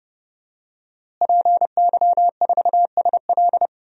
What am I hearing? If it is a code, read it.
Morse code, heard as a keyed tone, about 32 wpm, 700 Hz. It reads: PY4SL
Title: DXCC-100 Call Signs - 5 Characters - Repeated 3x - 30wpm